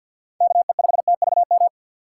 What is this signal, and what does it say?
Morse code, heard as a keyed tone, about 46 wpm, 700 Hz. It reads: KE5TVM
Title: USA Call Signs - 4x Character Spacing - 50wpm